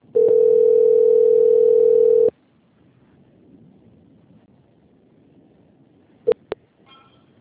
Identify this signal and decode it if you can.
Ringback tone (North America), 440 + 480 Hz.
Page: Itu - SP